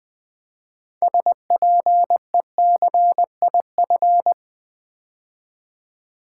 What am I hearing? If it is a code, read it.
Morse code, heard as a keyed tone, about 20 wpm, 700 Hz. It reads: SPECIF
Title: Build A Word - 6 Letters At A Time - 12-Letter Words - 20wpm